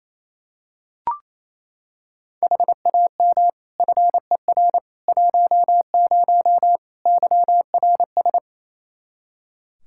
Morse code, heard as a keyed tone, about 28 wpm, 700 Hz. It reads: HAM FER 10 YRS
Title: K1USN SST QSO Practice 28wpm